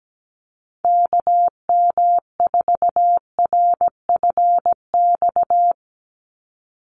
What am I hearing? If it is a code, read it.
Morse code, heard as a keyed tone, about 17 wpm, 700 Hz. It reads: KM4RFX
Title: USA Call Signs 17wpm